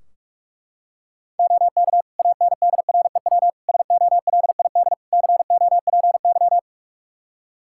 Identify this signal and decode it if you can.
Morse code, heard as a keyed tone, about 44 wpm, 700 Hz. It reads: OK ANDREW SOLID COPY
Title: Practice QSO - Exchange 3 - More Info - 45wpm